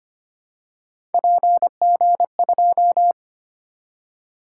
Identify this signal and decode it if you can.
Morse code, heard as a keyed tone, about 29 wpm, 700 Hz. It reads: PG2
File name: Speed-Racing - Three Letter-Number Combinations - 25wpm